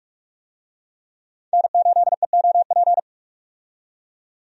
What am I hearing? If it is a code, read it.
Morse code, heard as a keyed tone, about 42 wpm, 700 Hz. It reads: N8EOP
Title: USA Call Signs - 5 Characters - 45wpm